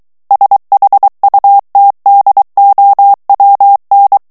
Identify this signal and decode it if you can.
Morse code, heard as a keyed tone, about 23 wpm, 800 Hz. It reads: SHUTDOWN
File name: ANGLAIS texte clair 3 vitesse 22 mots minute